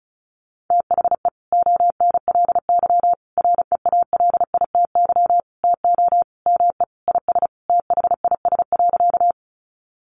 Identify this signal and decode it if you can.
Morse code, heard as a keyed tone, about 34 wpm, 700 Hz. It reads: THE ONLY REALITY TO ME IS THIS.
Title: Sentences from Top 1200 Words 35wpm